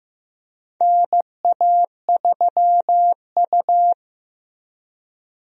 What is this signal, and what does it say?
Morse code, heard as a keyed tone, about 15 wpm, 700 Hz. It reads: NA3U
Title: USA Call Signs - 4 Characters - Repeated 3x - 15wpm